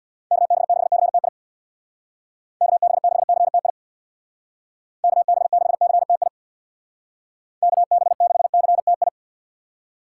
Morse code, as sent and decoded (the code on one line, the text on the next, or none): -.- -... -.... -.-. - .. / -.- -... -.... -.-. - .. / -.- -... -.... -.-. - .. / -.- -... -.... -.-. - ..
KB6CTI KB6CTI KB6CTI KB6CTI